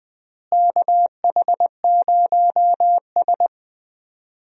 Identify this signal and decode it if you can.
Morse code, heard as a keyed tone, about 20 wpm, 700 Hz. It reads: KH0S
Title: USA Call Signs - 4 Characters - Repeated 3x - 20wpm